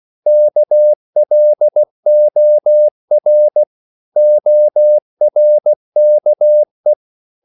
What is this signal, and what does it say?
Morse code, heard as a keyed tone, about 16 wpm, 600 Hz. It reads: KLOR ORKE
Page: Korte danske ord 16wpm